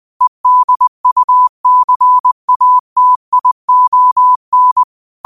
Morse code, heard as a keyed tone, about 20 wpm, 1000 Hz. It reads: EDUCATION
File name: qotdfiles / 20wpm